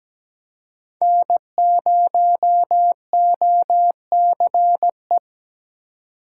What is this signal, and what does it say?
Morse code, heard as a keyed tone, about 17 wpm, 700 Hz. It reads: N0OCE
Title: USA Call Signs - 5 Characters - 2x Character Spacing - 17wpm